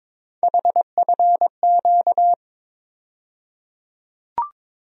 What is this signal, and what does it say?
Morse code, heard as a keyed tone, about 22 wpm, 700 Hz. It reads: HFQ
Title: USA Call Sign Postfixes - 3 Letters 22wpm